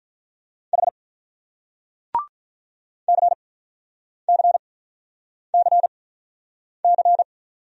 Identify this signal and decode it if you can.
Morse code, heard as a keyed tone, about 48 wpm, spaced as for 9 wpm, 700 Hz. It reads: H CCCC